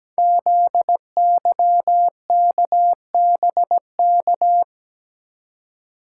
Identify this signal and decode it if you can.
Morse code, heard as a keyed tone, about 17 wpm, 700 Hz. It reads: ZYKBK
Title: Code Groups - 5 Letters - 17wpm